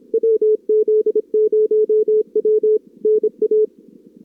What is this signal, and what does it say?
Morse code, heard as a keyed tone, about 34 wpm, 440 Hz. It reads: WZ0W NA
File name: CWSprint / Feb2022 / k5tr-feb-na-sprint-cw-k5ot -0000